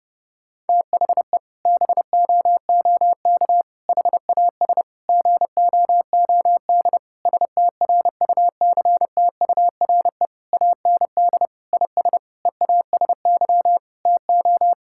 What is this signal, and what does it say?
Morse code, heard as a keyed tone, about 29 wpm, 700 Hz. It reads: THE BOOK HAS GOOD STRUCTURE AND IS EASY TO
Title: Sentences from Top 1000 Words - Review Part 6 - 30wpm